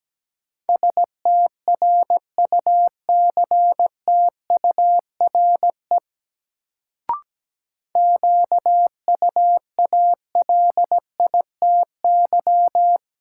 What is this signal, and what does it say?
Morse code, heard as a keyed tone, about 17 wpm, 700 Hz. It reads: STRUCTURE QUALITY
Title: Sentences from Top 1500 Words - New Word Drill Alt - 17wpm